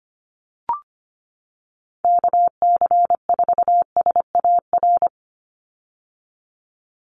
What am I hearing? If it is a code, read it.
Morse code, heard as a keyed tone, about 25 wpm, 700 Hz. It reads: KC4SAR
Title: USA Call Signs 25wpm